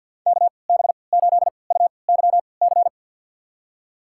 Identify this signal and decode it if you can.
Morse code, heard as a keyed tone, about 47 wpm, spaced as for 28 wpm, 700 Hz. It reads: KB8UYC
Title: USA Call Signs - 6 Characters - 3x Character Spacing - 50wpm